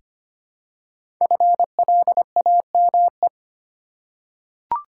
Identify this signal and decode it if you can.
Morse code, heard as a keyed tone, about 26 wpm, 700 Hz. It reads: FLAME
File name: 5 Letter Words - First 3 Letters Given - 25wpm